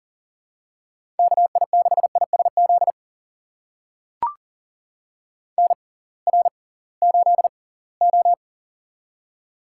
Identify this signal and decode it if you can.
Morse code, heard as a keyed tone, about 40 wpm, spaced as for 23 wpm, 700 Hz. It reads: KI6ISZ NR8O
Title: USA Call Signs - 6x Character Spacing - 40wpm